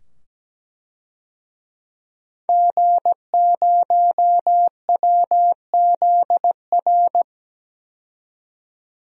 Morse code, heard as a keyed tone, about 17 wpm, 700 Hz. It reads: G0WZR
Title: DXCC-100 Call Signs - 5 Characters - 17wpm